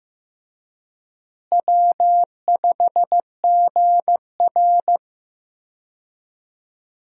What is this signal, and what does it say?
Morse code, heard as a keyed tone, about 15 wpm, 700 Hz. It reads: W5GR
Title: USA Call Signs - 4 Characters - Repeated 3x - 15wpm